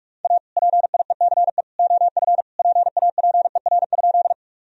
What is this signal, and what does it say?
Morse code, heard as a keyed tone, about 46 wpm, 700 Hz. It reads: A PIECE OF PAPER?
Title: Sentences from Top 1000 Words - Review Part 2 - 45wpm